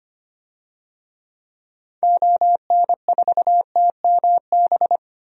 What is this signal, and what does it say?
Morse code, heard as a keyed tone, about 25 wpm, 700 Hz. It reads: ON4TMB
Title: DXCC-100 Call Signs - 6 Characters - Repeated 2x - 25wpm